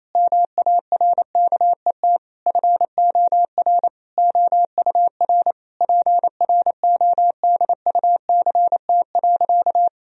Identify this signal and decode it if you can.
Morse code, heard as a keyed tone, about 28 wpm, 700 Hz. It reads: MARKET FOR OUR PRODUCT.
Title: Sentences from Top 1000 Words - Review Part 3 - 28wpm